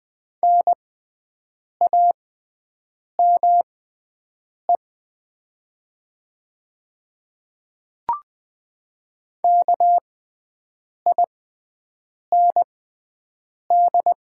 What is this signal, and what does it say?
Morse code, heard as a keyed tone, about 20 wpm, spaced as for 7 wpm, 700 Hz. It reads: NAME KIND
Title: 6x Character Spacing - Top 300 Words - 20wpm